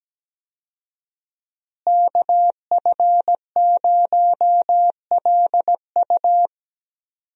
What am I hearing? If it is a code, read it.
Morse code, heard as a keyed tone, about 17 wpm, 700 Hz. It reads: KF0LU
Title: USA Call Signs - 5 Characters - Repeated 3x - 17wpm